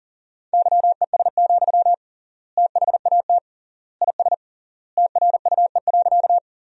Morse code, heard as a keyed tone, about 40 wpm, 700 Hz. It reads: YES, THAT IS TRUE.
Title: 3x Word-Spacing - Sentences from Top 500 Words - 40wpm